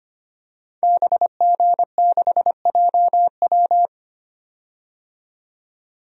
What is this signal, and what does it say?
Morse code, heard as a keyed tone, about 25 wpm, 700 Hz. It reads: BG6JW
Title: DXCC-100 Call Signs - 5 Characters - 25wpm